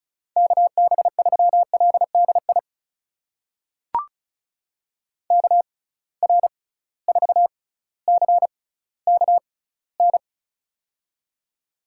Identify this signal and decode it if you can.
Morse code, heard as a keyed tone, about 36 wpm, spaced as for 18 wpm, 690 Hz. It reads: KB3LDI KR4CKN
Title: USA Call Signs - 6 Characters - 6x Character Spacing - 35wpm